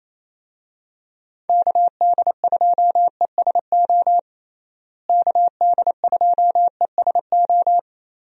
Morse code, heard as a keyed tone, about 28 wpm, 700 Hz. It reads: KD2ESO KD2ESO
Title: USA Call Signs - 6 Characters - Repeated 3x - 28wpm